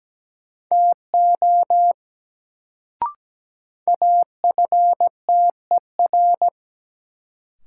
Morse code, heard as a keyed tone, about 17 wpm, 700 Hz. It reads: TO AFTER
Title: Sets of 2 Words - New Word Drill - 17wpm